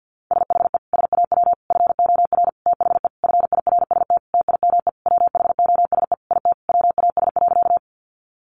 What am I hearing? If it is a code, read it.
Morse code, heard as a keyed tone, about 50 wpm, 700 Hz. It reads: SHE SAW FOR THE FIRST TIME WHOSE IT WAS.
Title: Sentences from Top 1200 Words 50wpm